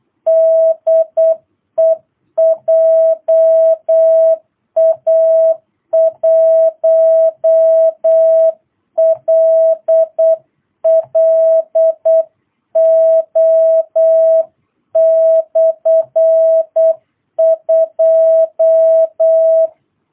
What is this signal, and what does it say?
Morse code, heard as a keyed tone, about 8 wpm, 660 Hz. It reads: DE JA1LLO/2
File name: jh8cbh / 2022cw